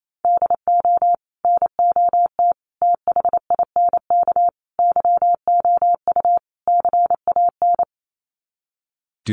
Morse code, heard as a keyed tone, about 28 wpm, 700 Hz. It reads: DO NOT THINK YOU CAN
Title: Sets of 5 Words 28wpm